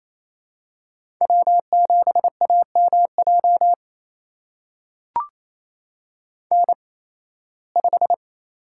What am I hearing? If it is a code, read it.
Morse code, heard as a keyed tone, about 28 wpm, spaced as for 18 wpm, 700 Hz. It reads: W7AMJ N5
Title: USA Call Signs - 5 Characters - 8x Character Spacing - 28wpm